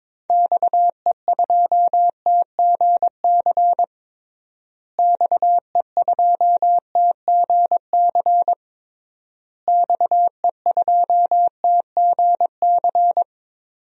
Morse code, heard as a keyed tone, about 22 wpm, 700 Hz. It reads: XE2TGC XE2TGC XE2TGC
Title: DXCC-100 Call Signs - 6 Characters - Repeated 3x - 22wpm